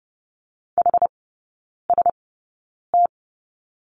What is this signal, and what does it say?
Morse code, heard as a keyed tone, about 30 wpm, spaced as for 9 wpm, 700 Hz. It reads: HST
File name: Lesson 16e - Letter c - Three Characters 30wpm